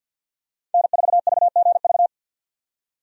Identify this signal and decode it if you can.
Morse code, heard as a keyed tone, about 51 wpm, 700 Hz. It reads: N4VGV
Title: USA Call Signs - 5 Characters - 50wpm